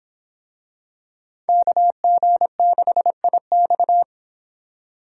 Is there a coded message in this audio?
Yes: Morse code from 1.5 to 4.0 s, heard as a keyed tone, about 24 wpm, 700 Hz. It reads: KG6IX